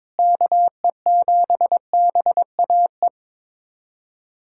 Morse code, heard as a keyed tone, about 22 wpm, 700 Hz. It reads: KE7BAE
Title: USA Callsign DE Callsign 22wpm